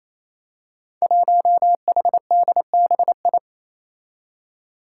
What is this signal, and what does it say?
Morse code, heard as a keyed tone, about 28 wpm, 700 Hz. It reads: 1HDBI
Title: Five Challenging Letter-Number Combinations 28wpm